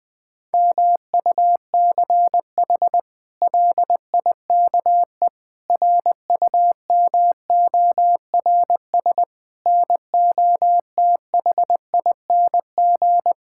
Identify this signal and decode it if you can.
Morse code, heard as a keyed tone, about 20 wpm, 700 Hz. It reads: MUCH LIKE RUMORS NOTHING
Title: Sentences with 8-Word Phrases - Spoken Beginning - 20wpm